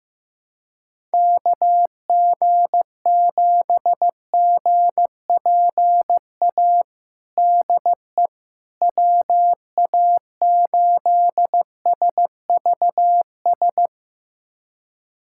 Morse code, heard as a keyed tone, about 15 wpm, 700 Hz. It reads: KG7GPA DE WA8SVS
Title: USA Callsign DE Callsign 15wpm